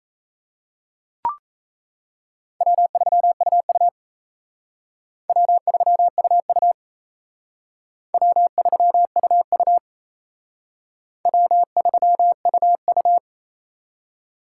Morse code, heard as a keyed tone, about 33 wpm, 700 Hz. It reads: W3UU W3UU W3UU W3UU
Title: Speed Racing - USA 1x2 Call Signs 28wpm